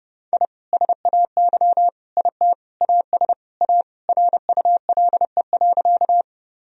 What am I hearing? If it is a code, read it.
Morse code, heard as a keyed tone, about 30 wpm, 700 Hz. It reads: I SAY IT AS A RULE.
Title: Sentences from Top 1000 Words - Review Part 2 - 30wpm